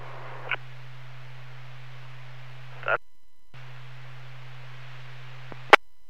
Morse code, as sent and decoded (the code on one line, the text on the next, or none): none